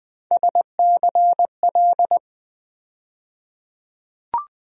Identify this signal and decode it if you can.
Morse code, heard as a keyed tone, about 20 wpm, 700 Hz. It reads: SCL
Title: USA Call Sign Postfixes - 3 Letters 20wpm